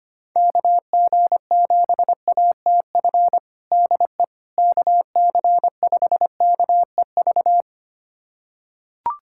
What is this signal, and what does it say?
Morse code, heard as a keyed tone, about 25 wpm, 700 Hz. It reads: KG7ATF DE KC5KEV
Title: USA Callsign DE Callsign - 6 Characters - 25wpm